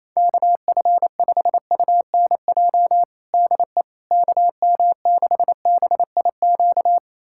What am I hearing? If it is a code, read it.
Morse code, heard as a keyed tone, about 28 wpm, 700 Hz. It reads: KF5UNJ DE KM6BIQ
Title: USA Callsign DE Callsign 28wpm